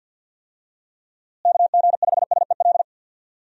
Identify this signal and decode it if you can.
Morse code, heard as a keyed tone, about 51 wpm, 700 Hz. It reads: KG5SEL
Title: USA Call Signs - 6 Characters - Repeated 3x - 50wpm